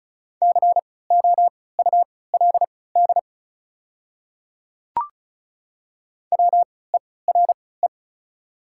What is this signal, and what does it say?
Morse code, heard as a keyed tone, about 36 wpm, spaced as for 20 wpm, 690 Hz. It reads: COULD WERE
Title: 3x Character Spacing - Top 200 Words - 35wpm